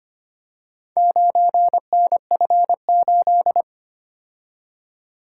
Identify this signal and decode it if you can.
Morse code, heard as a keyed tone, about 25 wpm, 700 Hz. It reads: 9NF8
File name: practice / 25wpm / Four Letter-Number Combinations - 25wpm